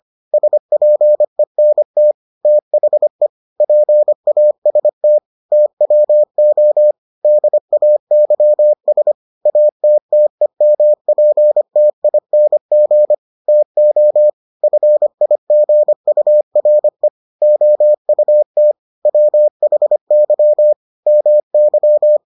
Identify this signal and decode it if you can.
Morse code, heard as a keyed tone, about 25 wpm, 600 Hz. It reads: SPENT THE PAST TWO DAYS ATTEMPTING TO FIGURE OUT WHY MY